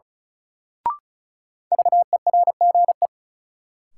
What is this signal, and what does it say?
Morse code, heard as a keyed tone, about 34 wpm, 700 Hz. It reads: VERGE
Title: Sets of 4 Words - New Word Drill - 35wpm